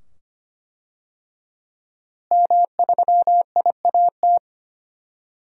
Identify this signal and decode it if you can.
Morse code, heard as a keyed tone, about 25 wpm, 700 Hz. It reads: M3IAT
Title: DXCC-100 Call Signs - 5 Characters - Repeated 2x - 25wpm